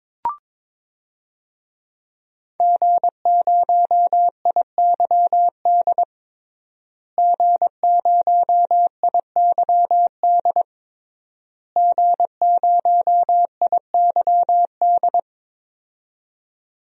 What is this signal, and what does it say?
Morse code, heard as a keyed tone, about 22 wpm, 700 Hz. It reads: G0IYD G0IYD G0IYD
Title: DXCC-100 Call Signs - 5 Characters - Repeated 3x - 22wpm